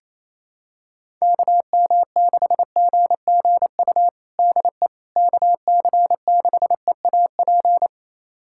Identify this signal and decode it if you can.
Morse code, heard as a keyed tone, about 28 wpm, 700 Hz. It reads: KM6GGU DE KC6EAP
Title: USA Callsign DE Callsign 28wpm